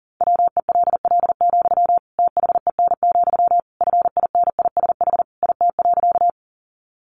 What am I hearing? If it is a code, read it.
Morse code, heard as a keyed tone, about 40 wpm, 700 Hz. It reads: WELL, THEN, FINISH IT.
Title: Sentences from Top 1200 Words 40wpm